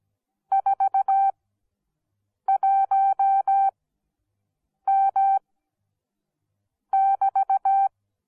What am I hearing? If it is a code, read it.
Morse code, heard as a keyed tone, about 17 wpm, spaced as for 6 wpm, 780 Hz. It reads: 41M<BT>